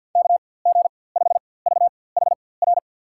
Morse code, heard as a keyed tone, about 48 wpm, spaced as for 23 wpm, 700 Hz. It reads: KG5VHR
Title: USA Call Signs - 6 Characters - 4x Character Spacing - 50wpm